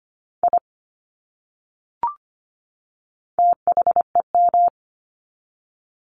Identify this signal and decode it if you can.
Morse code, heard as a keyed tone, about 25 wpm, 700 Hz. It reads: I THEM
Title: Top 200 Words 25wpm